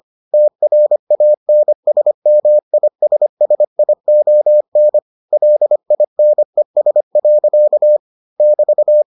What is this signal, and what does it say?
Morse code, heard as a keyed tone, about 25 wpm, 600 Hz. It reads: TRANSMISSION LINES. <BT>